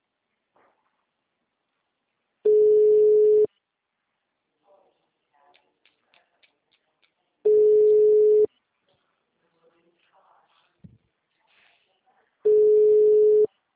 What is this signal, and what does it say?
Ringback tone (Europe, 425 Hz).
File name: out- -2028- - - .125